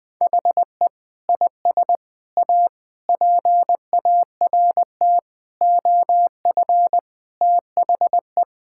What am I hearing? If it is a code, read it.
Morse code, heard as a keyed tone, about 21 wpm, 700 Hz. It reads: HE IS A PART OF THE
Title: Sentences with 4-Word Phrases - Spoken Beginning - 20wpm